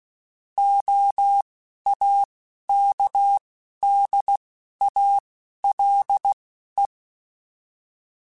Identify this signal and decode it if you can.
Morse code, heard as a keyed tone, about 16 wpm, spaced as for 11 wpm, 800 Hz. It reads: OAKDALE